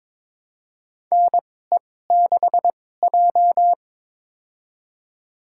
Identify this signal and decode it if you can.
Morse code, heard as a keyed tone, about 22 wpm, spaced as for 16 wpm, 700 Hz. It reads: NE6J
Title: USA Call Signs - 4 Characters - 2x Character Spacing - 22wpm